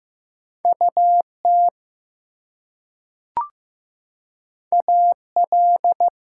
Morse code, heard as a keyed tone, about 15 wpm, 700 Hz. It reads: UT AL
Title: US States and Canadian Provinces 15wpm